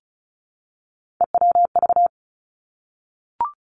Morse code, heard as a keyed tone, about 36 wpm, 700 Hz. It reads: EWV
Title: Three Challenging Letter-Number Combinations 35wpm